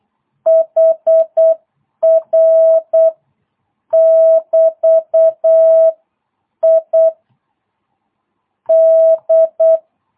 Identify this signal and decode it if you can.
Morse code, heard as a keyed tone, about 8 wpm, 660 Hz. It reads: HR <BT> I D